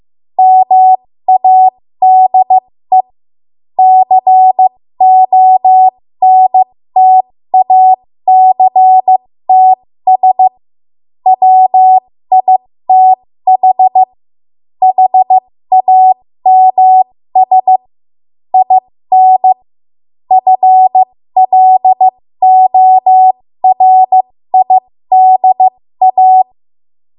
Morse code, heard as a keyed tone, about 15 wpm, spaced as for 13 wpm, 750 Hz. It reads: MADE CONTACTS WITH HAMS IN FLORIDA
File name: Archive / 13 WPM / 13WPM